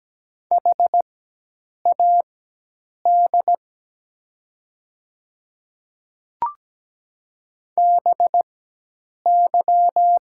Morse code, heard as a keyed tone, about 17 wpm, spaced as for 8 wpm, 700 Hz. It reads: HAD BY
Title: 4x Character Spacing - Top 100 Words - 17wpm